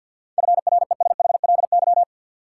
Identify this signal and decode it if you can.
Morse code, heard as a keyed tone, about 52 wpm, 700 Hz. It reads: FRESHLY
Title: Sets of 5 Words Encore - New Word Drill - 50wpm